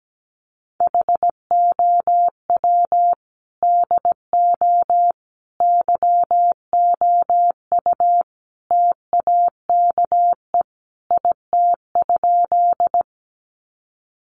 Morse code, heard as a keyed tone, about 17 wpm, 700 Hz. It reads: HOW DO YOU TAKE IT?
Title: Sentences from Top 200 Words 17wpm